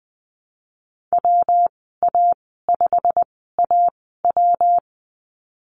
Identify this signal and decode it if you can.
Morse code, heard as a keyed tone, about 20 wpm, spaced as for 15 wpm, 700 Hz. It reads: WA5AW
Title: 167 Lesson 27g - Number 7 - Call Signs 20wpm